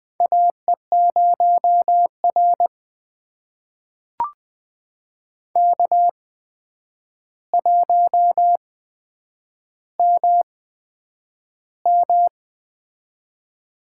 Morse code, heard as a keyed tone, about 20 wpm, spaced as for 9 wpm, 700 Hz. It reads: AE0R K1MM
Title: USA Call Signs - 4 Characters - 8x Character Spacing - 20wpm